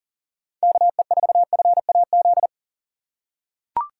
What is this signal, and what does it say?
Morse code, heard as a keyed tone, about 39 wpm, 700 Hz. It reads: KE4FAZ
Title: USA Call Signs - 6 Characters - 2x Character Spacing - 40wpm